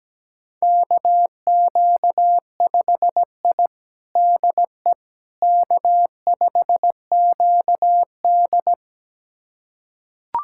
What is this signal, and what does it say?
Morse code, heard as a keyed tone, about 17 wpm, 700 Hz. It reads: KQ5I DE K5QD
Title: USA Callsign DE Callsign - 4 Characters - 17wpm